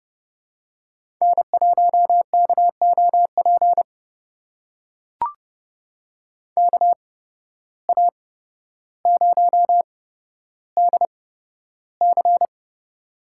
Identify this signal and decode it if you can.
Morse code, heard as a keyed tone, about 32 wpm, spaced as for 13 wpm, 700 Hz. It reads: N1KOP KA0DC
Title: USA Call Signs - 5 Characters - 8x Character Spacing - 30wpm